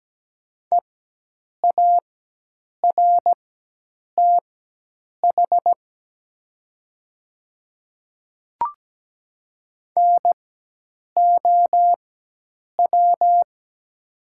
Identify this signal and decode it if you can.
Morse code, heard as a keyed tone, about 17 wpm, spaced as for 8 wpm, 700 Hz. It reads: EARTH NOW
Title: 4x Character Spacing - Top 200 Words - 17wpm